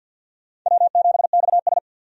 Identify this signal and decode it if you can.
Morse code, heard as a keyed tone, about 50 wpm, 700 Hz. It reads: W7XS
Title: USA Call Signs - 4 Characters - 50wpm